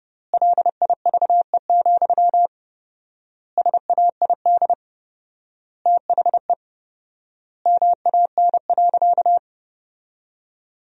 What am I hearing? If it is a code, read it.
Morse code, heard as a keyed tone, about 30 wpm, 700 Hz. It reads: LIVE, SAID THE MAN.
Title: 4x Word-Spacing - Sentences from Top 200 Words - 30wpm